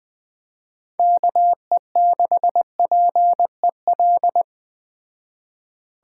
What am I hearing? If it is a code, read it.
Morse code, heard as a keyed tone, about 20 wpm, 700 Hz. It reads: KE6PEL
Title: Speed Racing - USA 2x3 Call Signs 17wpm